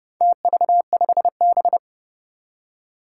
Morse code, heard as a keyed tone, about 31 wpm, 700 Hz. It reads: TV5B
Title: Four Challenging Letter-Number Combinations 30wpm